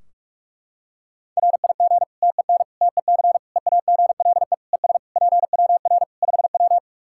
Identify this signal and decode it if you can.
Morse code, heard as a keyed tone, about 46 wpm, 700 Hz. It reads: RIG TEN TEC EAGLE ES PWR 5W
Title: Practice QSO - Exchange 2 - Info - 45wpm